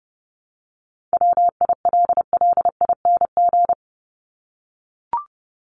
Morse code, heard as a keyed tone, about 30 wpm, 700 Hz. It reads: WILLING
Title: Top 751-1000 Words 30wpm